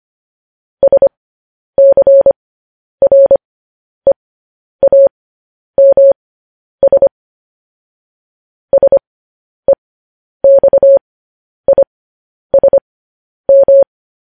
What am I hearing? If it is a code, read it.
Morse code, heard as a keyed tone, about 25 wpm, spaced as for 10 wpm, 550 Hz. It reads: SCREAMS SEXISM
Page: news-headlines-in-morse-code-at-10-wpm